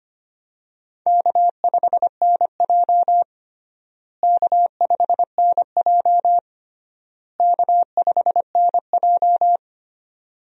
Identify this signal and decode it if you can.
Morse code, heard as a keyed tone, about 25 wpm, 700 Hz. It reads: K5NJ K5NJ K5NJ
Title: USA Call Signs - 4 Characters - Repeated 3x - 25wpm